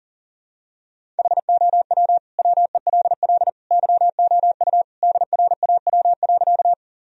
Morse code, heard as a keyed tone, about 40 wpm, 700 Hz. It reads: HOW WELL YOU DRAW.
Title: Sentences from Top 1000 Words - Review Part 1 - 40wpm